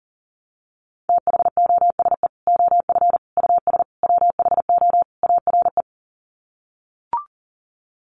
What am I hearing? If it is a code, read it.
Morse code, heard as a keyed tone, about 40 wpm, 700 Hz. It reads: THOSE OF US WHO ARE
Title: Sets of 5 Words Encore2 40wpm